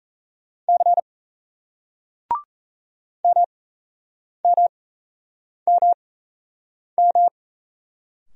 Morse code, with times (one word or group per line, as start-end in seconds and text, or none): 0.7-1.0 C
3.2-7.3 MMMM